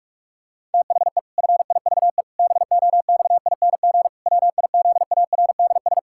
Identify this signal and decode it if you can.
Morse code, heard as a keyed tone, about 48 wpm, 700 Hz. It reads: THE FIVE BOXING WIZARDS
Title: Pangram Challenge v2 - 45wpm